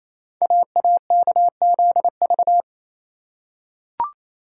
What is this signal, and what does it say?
Morse code, heard as a keyed tone, about 29 wpm, 700 Hz. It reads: AAKZV
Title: Code Groups - 5 Letter:Numbers - 28wpm